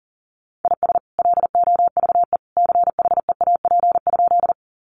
Morse code, heard as a keyed tone, about 40 wpm, 700 Hz. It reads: IS LOVE CHEAP?
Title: Sentences from Top 1200 Words 40wpm